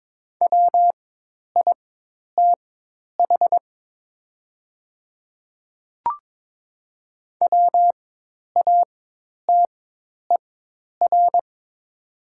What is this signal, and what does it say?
Morse code, heard as a keyed tone, about 22 wpm, spaced as for 10 wpm, 700 Hz. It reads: WITH WATER
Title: 4x Character Spacing - Top 200 Words - 22wpm